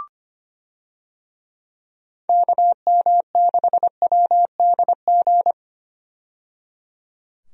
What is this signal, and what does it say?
Morse code, heard as a keyed tone, about 25 wpm, 700 Hz. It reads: KM6WDG
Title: USA Call Signs - 6 Characters - 25wpm